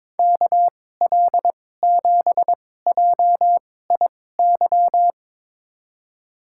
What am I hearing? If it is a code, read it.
Morse code, heard as a keyed tone, about 22 wpm, spaced as for 16 wpm, 700 Hz. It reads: KL7JIY
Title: USA Call Signs - 6 Characters - 2x Character Spacing - 22wpm